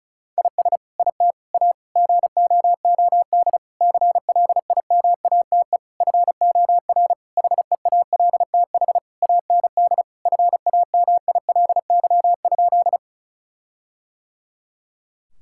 Morse code, heard as a keyed tone, about 34 wpm, 700 Hz. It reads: IS IT A GOOD CLIMATE FOR HEALTH AND FAMILY?
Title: Sentences from Top 1600 Words 35wpm